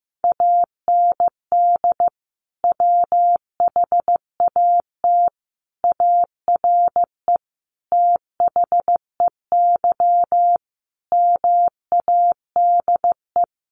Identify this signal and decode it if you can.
Morse code, heard as a keyed tone, about 15 wpm, 700 Hz. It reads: AND WHAT ARE THEY MADE
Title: Sentences from Top 200 Words 15wpm